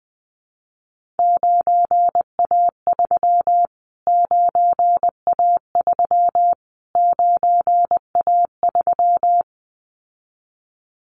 Morse code, heard as a keyed tone, about 20 wpm, 700 Hz. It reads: 9A3 9A3 9A3
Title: Callsign Prefixes for Top 25 DXCC - Repeated 3x - 20wpm